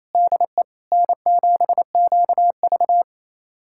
Morse code, heard as a keyed tone, about 27 wpm, 700 Hz. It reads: DE N7QV
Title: USA Callsign DE Callsign - 4 Characters - 28wpm